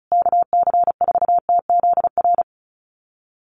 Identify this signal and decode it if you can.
Morse code, heard as a keyed tone, about 34 wpm, 700 Hz. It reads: KC4TZR
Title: USA Call Signs Repeated 3x 35wpm